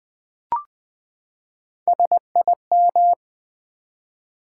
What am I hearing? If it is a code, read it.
Morse code, heard as a keyed tone, about 20 wpm, 700 Hz. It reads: SIM
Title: Build A Word - 3 Letters At A Time - 9-Letter Words - 20wpm